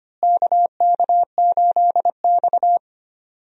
Morse code, heard as a keyed tone, about 25 wpm, 700 Hz. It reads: KK8X
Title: USA Call Signs - 4 Characters - Repeated 2x - 25wpm